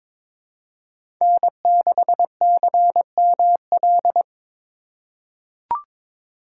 Morse code, heard as a keyed tone, about 22 wpm, 700 Hz. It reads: N6CML
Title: USA Call Signs - 5 Characters - 4x Character Spacing - 22wpm